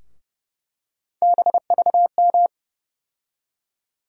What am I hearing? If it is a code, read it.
Morse code, heard as a keyed tone, about 30 wpm, 700 Hz. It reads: BVM
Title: USA Call Sign Postfixes - 3 Letters 30wpm